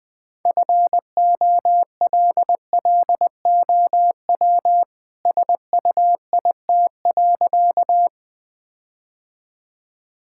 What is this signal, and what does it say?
Morse code, heard as a keyed tone, about 21 wpm, 700 Hz. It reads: FOLLOW SUIT.
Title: Sentences from Top 1600 Words 20wpm